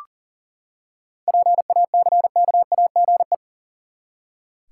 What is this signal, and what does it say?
Morse code, heard as a keyed tone, about 40 wpm, 700 Hz. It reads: PACKAGE
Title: Seven Letter Words - 40wpm